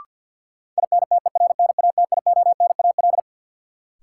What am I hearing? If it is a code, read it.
Morse code, heard as a keyed tone, about 45 wpm, 700 Hz. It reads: INTERNATIONAL
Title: Sentences from Top 1000 Words - Review Part 4 - New Word Drill - 50wpm